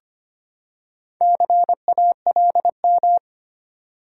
Morse code, heard as a keyed tone, about 25 wpm, 700 Hz. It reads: CALM
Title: Four Letter Words - 25wpm